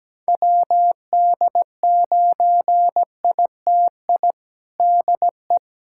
Morse code, heard as a keyed tone, about 17 wpm, 700 Hz. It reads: WD9ITI DE
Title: USA Callsign DE Callsign - 6 Characters - 17wpm